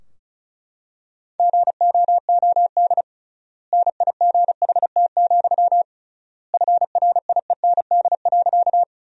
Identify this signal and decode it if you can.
Morse code, heard as a keyed tone, about 36 wpm, 700 Hz. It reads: GOOD NIGHT, FRIEND.
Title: 3x Word-Spacing - Sentences from Top 300 Words - 35wpm